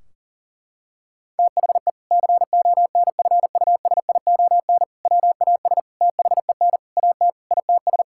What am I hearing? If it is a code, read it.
Morse code, heard as a keyed tone, about 37 wpm, 700 Hz. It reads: THE CONFUSION WAS THEN AT ITS
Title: Sentences from Top 1600 Words 40wpm